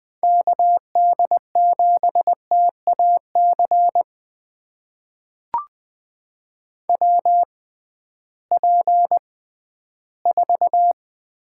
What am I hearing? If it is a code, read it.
Morse code, heard as a keyed tone, about 20 wpm, spaced as for 13 wpm, 700 Hz. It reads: KD7TAC WP4
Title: USA Call Signs - 6 Characters - 6x Character Spacing - 20wpm